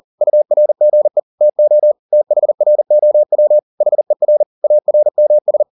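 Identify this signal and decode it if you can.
Morse code, heard as a keyed tone, about 40 wpm, 600 Hz. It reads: URGE TO THROW HER ARMS
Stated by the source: Morse code at 40 wpm, practice text, CW med Gnister.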